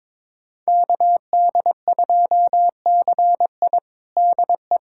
Morse code, heard as a keyed tone, about 22 wpm, 700 Hz. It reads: KD2CI DE
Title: USA Callsign DE Callsign - 5 Characters - 22wpm